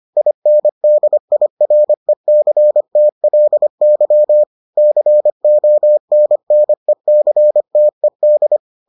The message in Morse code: INDIRECTLY CONNECTED

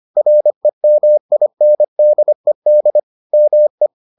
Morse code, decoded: REMINDED ME

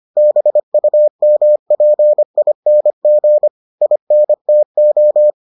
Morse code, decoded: BUMPING INTO

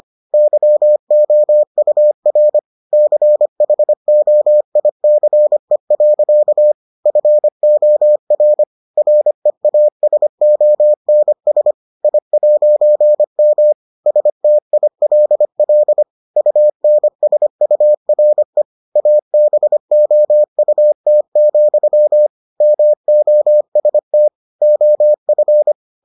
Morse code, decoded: YOUR CHOICE. FOR REASONS I'M STILL UNSURE ABOUT, MOST OF

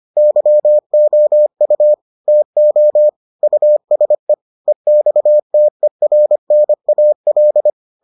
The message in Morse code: YOU TO USE EXTERNAL

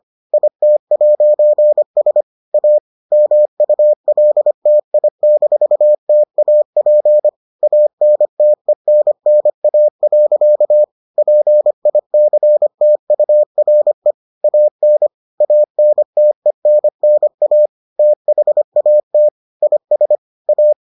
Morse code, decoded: IT'S A MULTI-TAP ANTENNA. PICTURE AN ANTENNA THAT IS A

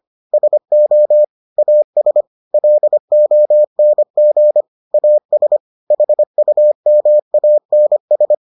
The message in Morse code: SO AS LONG AS HUMANS

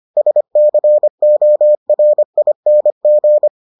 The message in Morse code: SCORING

